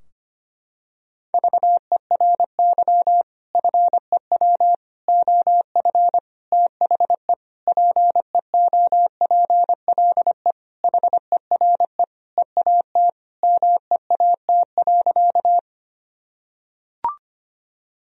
Morse code, heard as a keyed tone, about 25 wpm, 700 Hz. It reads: VERY FEW OF THE PEOPLE HERE EAT MEAT.
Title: Sentences from Top 1000 Words - Review Part 3 - 25wpm